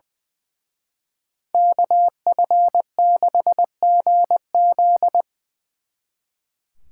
Morse code, heard as a keyed tone, about 24 wpm, 700 Hz. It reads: KF6GZ
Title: Speed Racing - USA 2x2 Call Signs 20wpm